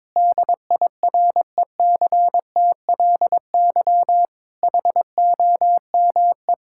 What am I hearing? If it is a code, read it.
Morse code, heard as a keyed tone, about 21 wpm, 700 Hz. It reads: DIRECTLY HOME
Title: Sentences from Top 1000 Words - Review Part 5 - 22wpm